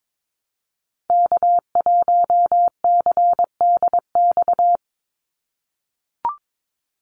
Morse code, heard as a keyed tone, about 22 wpm, 700 Hz. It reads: K1CDX
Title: Call Signs - 22wpm